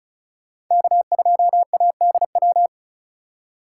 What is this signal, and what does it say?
Morse code, heard as a keyed tone, about 44 wpm, 700 Hz. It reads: K2ADW